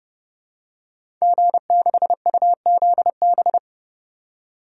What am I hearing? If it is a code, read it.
Morse code, heard as a keyed tone, about 30 wpm, 700 Hz. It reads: G6UZB
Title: DXCC-100 Call Signs - 5 Characters - 30wpm